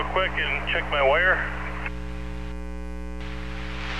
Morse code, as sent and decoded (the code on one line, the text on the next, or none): none